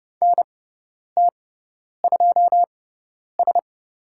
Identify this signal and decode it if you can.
Morse code, heard as a keyed tone, about 30 wpm, spaced as for 10 wpm, 700 Hz. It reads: NT2S
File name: Lesson 18g - Letter m - Call Signs 30wpm